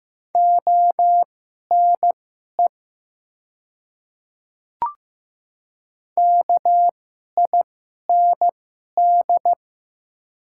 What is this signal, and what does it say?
Morse code, heard as a keyed tone, about 15 wpm, spaced as for 11 wpm, 700 Hz. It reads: ONE KIND
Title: 2x Character Spacing - Top 200 Words - 15wpm